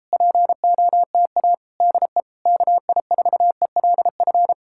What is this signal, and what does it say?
Morse code, heard as a keyed tone, about 37 wpm, 700 Hz. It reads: POTA DE KI4ELF